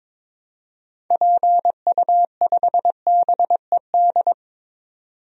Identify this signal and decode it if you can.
Morse code, heard as a keyed tone, about 21 wpm, 700 Hz. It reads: PU5BED
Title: DXCC-100 Call Signs - 6 Characters - 22wpm